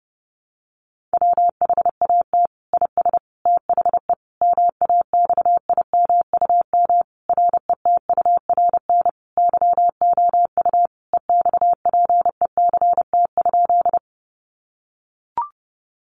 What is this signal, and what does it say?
Morse code, heard as a keyed tone, about 28 wpm, 700 Hz. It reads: WHAT IS THE MAXIMUM RETURN YOU EXPECT?
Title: Sentences from Top 1300 Words 30wpm